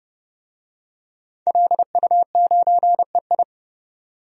Morse code, heard as a keyed tone, about 30 wpm, 700 Hz. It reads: LU9EI
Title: DXCC-100 Call Signs - 5 Characters - 30wpm